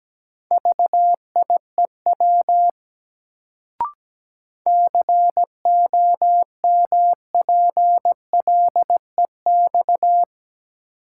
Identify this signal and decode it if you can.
Morse code, heard as a keyed tone, about 17 wpm, 700 Hz. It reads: VIEW COMPLEX
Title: Sentences from Top 900 Words - New Word Drill - 17wpm